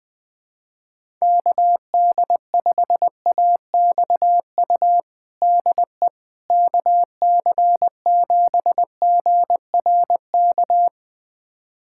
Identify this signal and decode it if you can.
Morse code, heard as a keyed tone, about 20 wpm, 700 Hz. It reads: KD5AXU DE KC7GRK
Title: USA Callsign DE Callsign - 6 Characters - 20wpm